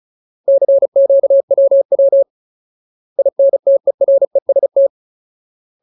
Morse code, heard as a keyed tone, about 34 wpm, 550 Hz. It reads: CQWW INTEREST
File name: Ultimate Word List / 35WPM Ultimate